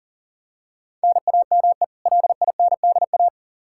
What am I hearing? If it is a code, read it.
Morse code, heard as a keyed tone, about 40 wpm, 700 Hz. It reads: NAME LINDA
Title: Practice QSO - Exchange 1 - Beginning - 40wpm